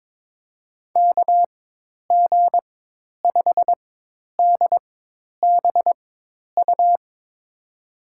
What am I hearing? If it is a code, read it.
Morse code, heard as a keyed tone, about 22 wpm, spaced as for 10 wpm, 700 Hz. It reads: KG5DBU
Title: USA Call Signs - 6 Characters - 4x Character Spacing - 22wpm